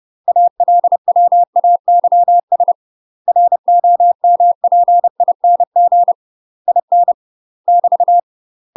Morse code, heard as a keyed tone, about 30 wpm, 700 Hz. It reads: ALWAYS ROMPING IN <BT>
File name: UK MASTER QSOs TRAINING / 30WPM / UK MASTER QSOs 0024